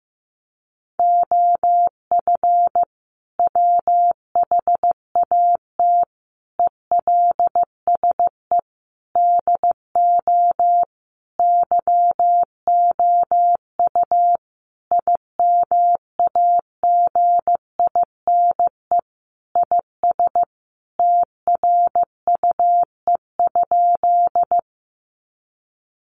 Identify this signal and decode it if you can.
Morse code, heard as a keyed tone, about 15 wpm, 700 Hz. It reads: OF WHAT ELSE DO YOU IMAGINE IS TRUE?
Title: Sentences from Top 1400 Words 15wpm